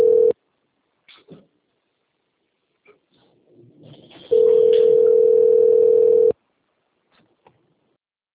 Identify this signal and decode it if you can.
Ringback tone (North America), 440 + 480 Hz.
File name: rg-1600-unknown- - - .51959